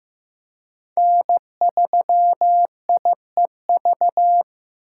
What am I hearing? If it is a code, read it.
Morse code, heard as a keyed tone, about 15 wpm, 700 Hz. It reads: N3IEV
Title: USA Call Signs - 5 Characters - 3x Character Spacing - 15wpm